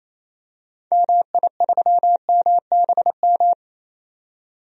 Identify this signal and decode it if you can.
Morse code, heard as a keyed tone, about 28 wpm, 700 Hz. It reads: MI3MBM
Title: DXCC-100 Call Signs - 6 Characters - 28wpm